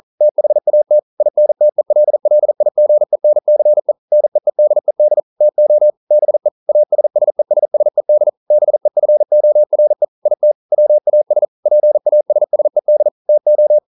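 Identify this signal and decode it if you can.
Morse code, heard as a keyed tone, about 41 wpm, 600 Hz. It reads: THAT INTELLIGENCE NEEDED TO BE ASSESSED BEFORE IT WAS PASSED TO
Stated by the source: Morse code practice at 41 wpm, from CW med Gnister.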